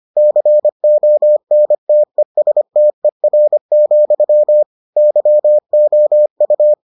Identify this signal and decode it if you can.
Morse code, heard as a keyed tone, about 25 wpm, 600 Hz. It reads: CONTESTER, YOU